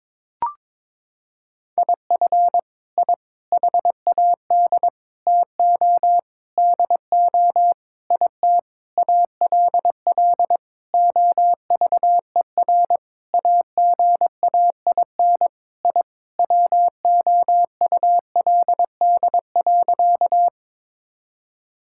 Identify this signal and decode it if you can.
Morse code, heard as a keyed tone, about 22 wpm, 700 Hz. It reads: IF I HAD TO DO IT ALL OVER AGAIN I WOULD.
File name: Sentences with 9-Word Phrases - 22wpm